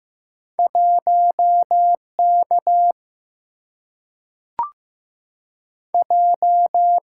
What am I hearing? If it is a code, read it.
Morse code, heard as a keyed tone, about 15 wpm, 700 Hz. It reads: 1 K J
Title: Single-Letter Single-Number 15wpm